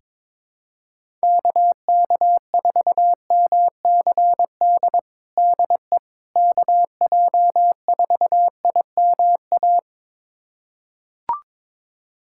Morse code, heard as a keyed tone, about 22 wpm, 700 Hz. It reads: KK4MCD DE KJ4IMA
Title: USA Callsign DE Callsign - 6 Characters - 22wpm